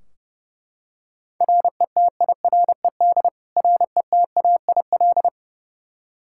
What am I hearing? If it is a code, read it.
Morse code, heard as a keyed tone, about 32 wpm, 700 Hz. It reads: RETIRED RETAIL
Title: K1USN SST QSO Practice 30wpm